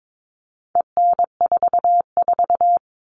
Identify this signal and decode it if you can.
Morse code, heard as a keyed tone, about 22 wpm, 700 Hz. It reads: EN44
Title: All Cut Numbers - 4-Digit 22wpm